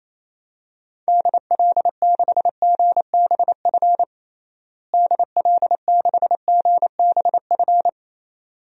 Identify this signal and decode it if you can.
Morse code, heard as a keyed tone, about 28 wpm, 700 Hz. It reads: DL6GBF DL6GBF
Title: DXCC-100 Call Signs - 6 Characters - Repeated 2x - 28wpm